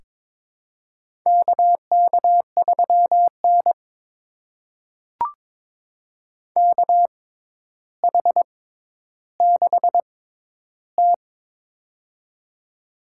Morse code, heard as a keyed tone, about 22 wpm, spaced as for 11 wpm, 700 Hz. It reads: KK3N KH6T
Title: USA Call Signs - 4 Characters - 6x Character Spacing - 22wpm